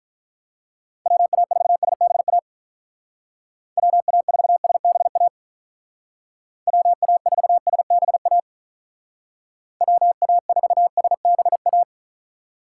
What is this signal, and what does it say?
Morse code, heard as a keyed tone, about 44 wpm, 700 Hz. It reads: WA4SBA WA4SBA WA4SBA WA4SBA